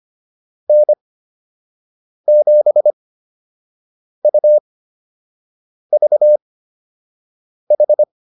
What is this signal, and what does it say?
Morse code, heard as a keyed tone, about 25 wpm, spaced as for 6 wpm, 600 Hz. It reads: N7UVH